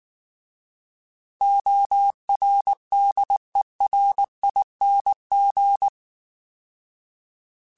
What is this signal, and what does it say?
Morse code, heard as a keyed tone, about 19 wpm, 800 Hz. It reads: ORDERING
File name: Alphabetical / Alphabetical mp3s / Os @18